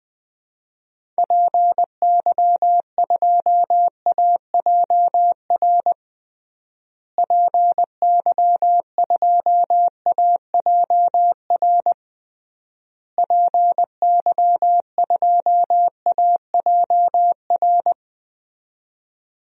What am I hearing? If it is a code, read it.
Morse code, heard as a keyed tone, about 20 wpm, 700 Hz. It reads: PY2AJR PY2AJR PY2AJR
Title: DXCC-100 Call Signs - 6 Characters - Repeated 3x - 20wpm